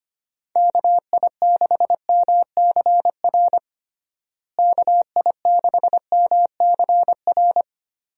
Morse code, heard as a keyed tone, about 25 wpm, 700 Hz. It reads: KI6MCR KI6MCR
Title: USA Call Signs - 6 Characters - Repeated 2x - 25wpm